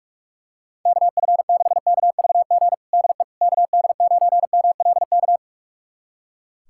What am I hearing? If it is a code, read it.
Morse code, heard as a keyed tone, about 41 wpm, 700 Hz. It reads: KF6KVG DE KD9MLK
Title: USA Callsign DE Callsign 45wpm